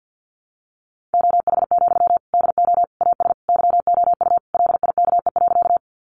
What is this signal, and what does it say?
Morse code, heard as a keyed tone, about 50 wpm, 700 Hz. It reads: OH, DO AS YOU LIKE.
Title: Sentences from Top 500 Words 50wpm